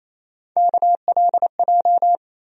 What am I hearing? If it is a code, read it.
Morse code, heard as a keyed tone, about 28 wpm, 700 Hz. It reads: KLJ
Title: USA Call Sign Postfixes - 3 Letters 28wpm